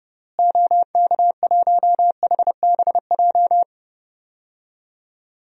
Morse code, heard as a keyed tone, about 30 wpm, 700 Hz. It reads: OK1HBJ
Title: DXCC-100 Call Signs - 6 Characters - Repeated 3x - 30wpm